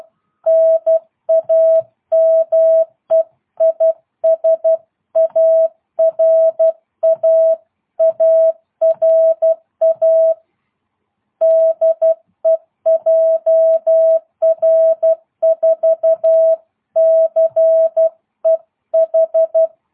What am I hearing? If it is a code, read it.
Morse code, heard as a keyed tone, about 12 wpm, 660 Hz. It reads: NAME IS ARA ARA DE JR4CEH
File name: jh8cbh / 2022cw